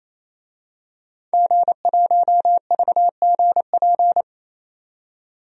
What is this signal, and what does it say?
Morse code, heard as a keyed tone, about 28 wpm, 700 Hz. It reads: G1VGP
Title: DXCC-100 Call Signs - 5 Characters - 28wpm